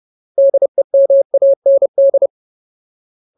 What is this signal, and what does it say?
Morse code, heard as a keyed tone, about 30 wpm, 550 Hz. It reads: DEMAND
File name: Ultimate Word List / 30WPM Ultimate